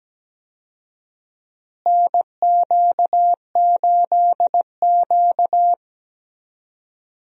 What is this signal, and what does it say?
Morse code, heard as a keyed tone, about 17 wpm, 700 Hz. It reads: NQ8Q
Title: USA Call Signs - 4 Characters - Repeated 2x - 17wpm